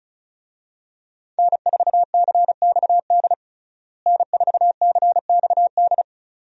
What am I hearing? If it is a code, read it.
Morse code, heard as a keyed tone, about 36 wpm, 700 Hz. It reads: N4CXD N4CXD
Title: USA Call Signs - 5 Characters - Repeated 2x - 35wpm